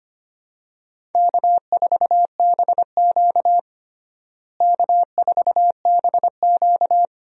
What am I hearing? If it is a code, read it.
Morse code, heard as a keyed tone, about 25 wpm, 700 Hz. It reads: K4BQ K4BQ
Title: USA Call Signs - 4 Characters - Repeated 2x - 25wpm